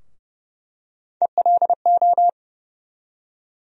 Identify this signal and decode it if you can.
Morse code, heard as a keyed tone, about 30 wpm, 700 Hz. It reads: ELO
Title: USA Call Sign Postfixes - 3 Letters 30wpm